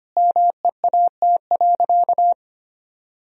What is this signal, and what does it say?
Morse code, heard as a keyed tone, about 25 wpm, 700 Hz. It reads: MEAT.
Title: Sentences from Top 1000 Words - Review Part 3 - 25wpm